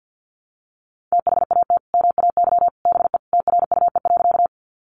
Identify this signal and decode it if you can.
Morse code, heard as a keyed tone, about 50 wpm, 700 Hz. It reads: THAT MAY BE TRUE.
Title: Sentences from Top 400 Words 50wpm